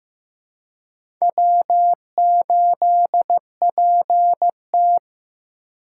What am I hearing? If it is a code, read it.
Morse code, heard as a keyed tone, about 15 wpm, 700 Hz. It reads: W8PT
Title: USA Call Signs - 4 Characters - 3x Character Spacing - 15wpm